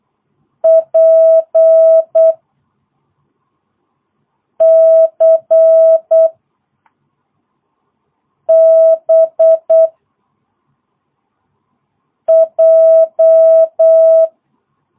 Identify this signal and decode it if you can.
Morse code, heard as a keyed tone, about 8 wpm, 650 Hz. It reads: P C B J
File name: jh8cbh / 2022cw